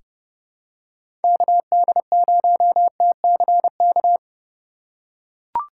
Morse code, heard as a keyed tone, about 31 wpm, 700 Hz. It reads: KD0TCK
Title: USA Call Signs - 3x Character Spacing - 30wpm